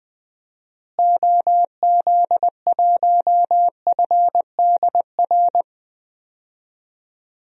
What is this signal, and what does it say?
Morse code, heard as a keyed tone, about 20 wpm, 700 Hz. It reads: OZ1FDR
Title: DXCC-100 Call Signs - 6 Characters - Repeated 2x - 20wpm